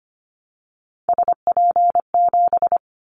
Morse code, heard as a keyed tone, about 25 wpm, 700 Hz. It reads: SP7
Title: Callsign Prefixes for Top 25 DXCC - Repeated 3x - 25wpm